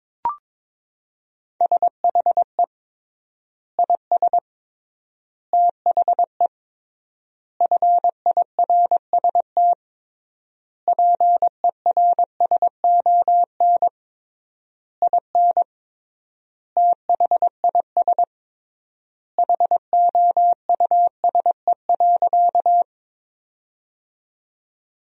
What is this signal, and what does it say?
Morse code, heard as a keyed tone, about 22 wpm, 700 Hz. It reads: SHE IS THE FIRST PERSON IN THIS HOUSE.
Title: 3x Word-Spacing - Sentences from Top 400 Words - 22wpm